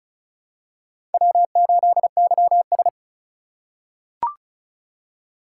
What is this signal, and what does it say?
Morse code, heard as a keyed tone, about 36 wpm, 690 Hz. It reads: W8YS
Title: USA Call Signs - 4 Characters - 2x Character Spacing - 35wpm